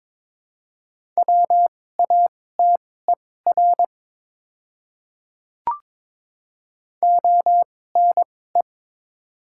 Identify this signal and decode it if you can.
Morse code, heard as a keyed tone, about 22 wpm, spaced as for 16 wpm, 690 Hz. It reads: WATER ONE
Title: 2x Character Spacing - Top 100 Words - 22wpm